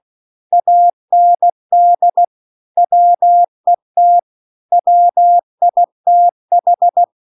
Morse code, heard as a keyed tone, about 16 wpm, 700 Hz. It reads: AND WET WITH
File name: 16WPM